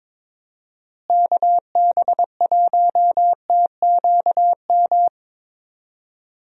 Morse code, heard as a keyed tone, about 22 wpm, 700 Hz. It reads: KB1TQM
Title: USA Call Signs - 6 Characters - Repeated 3x - 22wpm